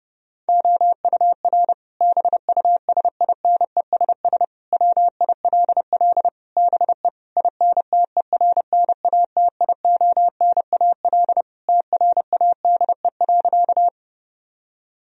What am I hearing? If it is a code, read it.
Morse code, heard as a keyed tone, about 30 wpm, 700 Hz. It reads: OUR BUSINESS WILL BE INTERNATIONAL TRADE.
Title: Sentences from Top 1000 Words - Review Part 4 - 30wpm